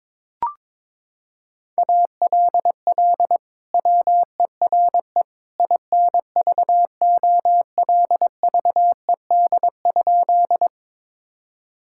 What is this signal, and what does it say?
Morse code, heard as a keyed tone, about 21 wpm, 700 Hz. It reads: ALL WERE INVOLVED?
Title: Sentences from Top 1000 Words - Review Part 5 - 22wpm